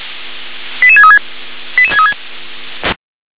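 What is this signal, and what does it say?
Five-tone selective call (ZVEI), heard as quick runs of five tones, tones of about 70 ms. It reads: 88636
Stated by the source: ZVEI five-tone selective call 88742 received on SDR.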